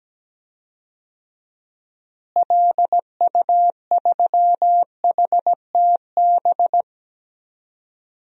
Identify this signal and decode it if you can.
Morse code, heard as a keyed tone, about 17 wpm, 700 Hz. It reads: LU3HTB
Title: DXCC-100 Call Signs - 6 Characters - Repeated 3x - 17wpm